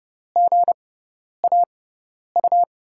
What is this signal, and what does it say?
Morse code, heard as a keyed tone, about 30 wpm, spaced as for 10 wpm, 700 Hz. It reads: GAU
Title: USA Call Signs - 6 Characters - 6x Character Spacing - 30wpm